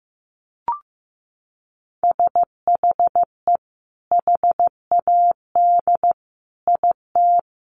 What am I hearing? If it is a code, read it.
Morse code, heard as a keyed tone, about 14 wpm, 700 Hz. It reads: SHE HAD IT
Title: Sentences from Top 1100 Words 15wpm